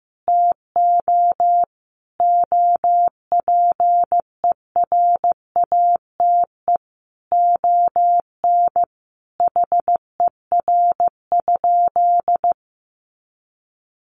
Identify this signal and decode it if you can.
Morse code, heard as a keyed tone, about 15 wpm, 700 Hz. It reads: TO OPERATE ON HER?
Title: Sentences from Top 1500 Words 15wpm